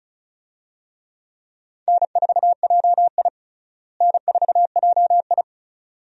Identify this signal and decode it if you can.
Morse code, heard as a keyed tone, about 36 wpm, 700 Hz. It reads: N4JI N4JI
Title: USA Call Signs - 4 Characters - Repeated 2x - 35wpm